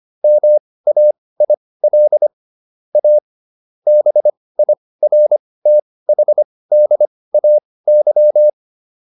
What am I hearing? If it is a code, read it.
Morse code, heard as a keyed tone, about 25 wpm, spaced as for 18 wpm, 600 Hz. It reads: MAIL A BIRTHDAY